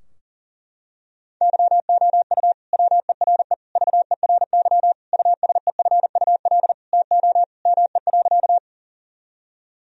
Morse code, heard as a keyed tone, about 40 wpm, 700 Hz. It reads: YOU WERE VERY USEFUL TO ME.
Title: Sentences from Top 1000 Words - Review Part 4 - 40wpm